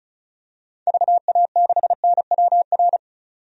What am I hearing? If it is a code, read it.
Morse code, heard as a keyed tone, about 34 wpm, 700 Hz. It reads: VA6NWR
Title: DXCC-100 Call Signs - 6 Characters - 35wpm